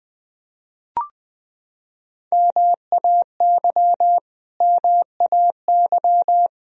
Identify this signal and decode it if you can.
Morse code, heard as a keyed tone, about 21 wpm, 700 Hz. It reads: MAY MAY
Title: Top 100 Words Repeated 20wpm